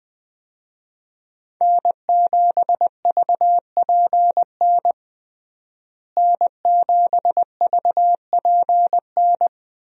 Morse code, heard as a keyed tone, about 21 wpm, 700 Hz. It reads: N7VPN N7VPN
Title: USA Call Signs - 5 Characters - Repeated 2x - 20wpm